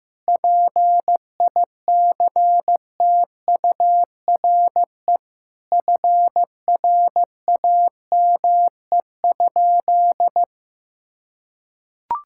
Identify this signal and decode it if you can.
Morse code, heard as a keyed tone, about 15 wpm, 700 Hz. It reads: PICTURE FRAME?
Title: Sentences from Top 1000 Words - Review Part 6 - 15wpm